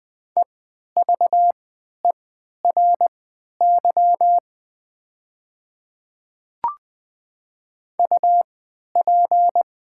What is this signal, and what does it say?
Morse code, heard as a keyed tone, about 20 wpm, spaced as for 11 wpm, 700 Hz. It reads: EVERY UP
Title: 3x Character Spacing - Top 200 Words - 20wpm